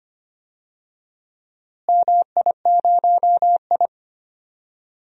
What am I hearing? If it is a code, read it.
Morse code, heard as a keyed tone, about 25 wpm, 700 Hz. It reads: MI0I
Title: DXCC-100 Call Signs - 4 Characters - Repeated 2x - 25wpm